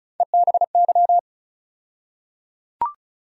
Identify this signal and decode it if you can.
Morse code, heard as a keyed tone, about 34 wpm, 700 Hz. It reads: EBY
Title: USA Call Sign Postfixes - 3 Letters 35wpm